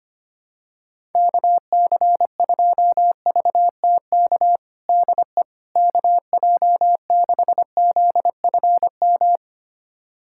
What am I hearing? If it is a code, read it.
Morse code, heard as a keyed tone, about 25 wpm, 700 Hz. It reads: KC2VTK DE KJ6ZFM
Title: USA Callsign DE Callsign 25wpm